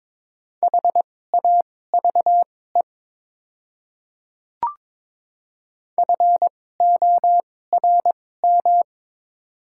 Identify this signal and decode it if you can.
Morse code, heard as a keyed tone, about 22 wpm, spaced as for 16 wpm, 700 Hz. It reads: HAVE FORM
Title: 2x Character Spacing - Top 200 Words - 22wpm